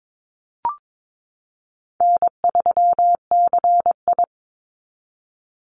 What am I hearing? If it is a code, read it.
Morse code, heard as a keyed tone, about 22 wpm, 700 Hz. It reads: N3CI
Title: Easy Call Signs Repeated 2x - 22wpm